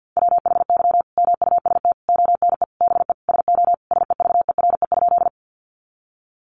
Morse code, heard as a keyed tone, about 50 wpm, 700 Hz. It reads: WHY MUST ONE BE SO SEVERE?
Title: Sentences from Top 1400 Words 50wpm